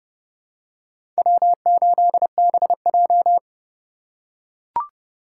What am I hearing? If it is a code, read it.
Morse code, heard as a keyed tone, about 30 wpm, 700 Hz. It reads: W8BJ
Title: USA Call Signs - 4 Characters - 3x Character Spacing - 30wpm